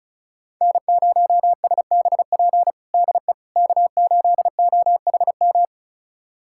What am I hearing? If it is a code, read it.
Morse code, heard as a keyed tone, about 34 wpm, 700 Hz. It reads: N0SBP DE K8OHM
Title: USA Callsign DE Callsign - 5 Characters - 35wpm